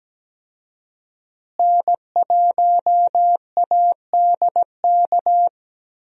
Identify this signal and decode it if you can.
Morse code, heard as a keyed tone, about 17 wpm, 700 Hz. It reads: N1ADK
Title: USA Call Signs - 5 Characters - Repeated 3x - 17wpm